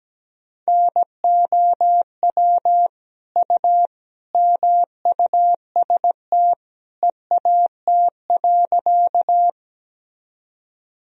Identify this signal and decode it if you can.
Morse code, heard as a keyed tone, about 17 wpm, 700 Hz. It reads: NOW U MUST EAT.
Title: Sentences from Top 300 Words and Abbreviations 17wpm